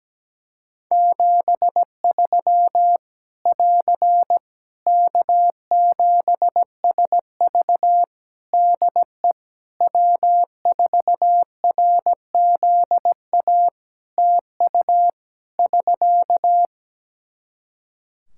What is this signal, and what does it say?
Morse code, heard as a keyed tone, about 17 wpm, 700 Hz. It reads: 73 <AR> K7SV DE W4RZA TU <SK>
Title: Practice QSO - Exchange 4 - Ending - 17wpm